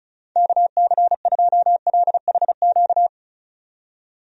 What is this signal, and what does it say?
Morse code, heard as a keyed tone, about 36 wpm, 700 Hz. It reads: KC2LHQ
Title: USA Call Signs - 6 Characters - 35wpm